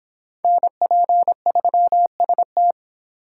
Morse code, heard as a keyed tone, about 30 wpm, 700 Hz. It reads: NP3ST